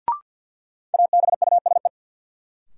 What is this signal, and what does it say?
Morse code, heard as a keyed tone, about 49 wpm, 700 Hz. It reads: ABUSE
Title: Sentences from Top 1300 Words - New Word Drill - 50wpm